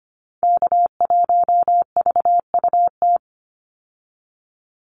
Morse code, heard as a keyed tone, about 25 wpm, 700 Hz. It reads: K1VUT
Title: Call Signs Repeated 2x - 25wpm